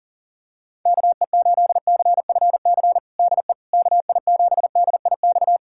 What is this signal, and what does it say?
Morse code, heard as a keyed tone, about 37 wpm, 700 Hz. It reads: KE8CFC DE KI7DIX
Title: USA Callsign DE Callsign - 6 Characters - 40wpm